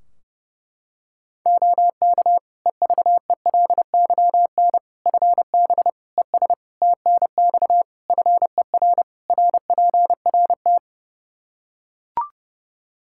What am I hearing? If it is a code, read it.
Morse code, heard as a keyed tone, about 30 wpm, 700 Hz. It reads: OK EVELYN FB ES TNX FER RPRT
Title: Practice QSO - Exchange 2 - Info - 30wpm